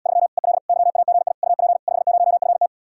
Morse code, heard as a keyed tone, about 79 wpm, 700 Hz. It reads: 3 EL YAGI UP 50FT